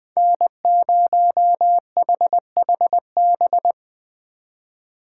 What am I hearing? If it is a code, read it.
Morse code, heard as a keyed tone, about 20 wpm, 700 Hz. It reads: N0HHB
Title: Speed Racing - USA 1x3 Call Signs 17wpm